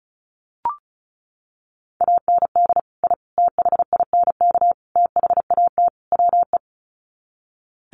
Morse code, heard as a keyed tone, about 34 wpm, 700 Hz. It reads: AND I THINK THAT WE
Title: Sets of 5 Words Encore2 35wpm